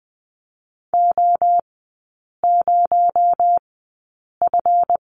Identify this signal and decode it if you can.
Morse code, heard as a keyed tone, about 20 wpm, spaced as for 8 wpm, 700 Hz. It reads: O0F
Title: 218 Lesson 35e - Number 0 - Three Characters 20wpm